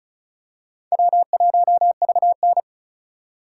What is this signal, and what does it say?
Morse code, heard as a keyed tone, about 34 wpm, 700 Hz. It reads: W1VN
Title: USA Call Signs - 4 Characters - 4x Character Spacing - 35wpm